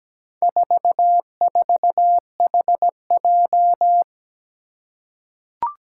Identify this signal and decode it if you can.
Morse code, heard as a keyed tone, about 17 wpm, 700 Hz. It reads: 44HJ
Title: Four Challenging Letter-Number Combinations 17wpm